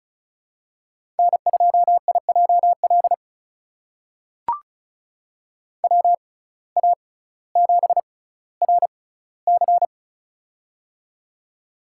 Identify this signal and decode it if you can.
Morse code, heard as a keyed tone, about 35 wpm, spaced as for 18 wpm, 700 Hz. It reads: N2IJL WA7RC
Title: USA Call Signs - 5 Characters - 6x Character Spacing - 35wpm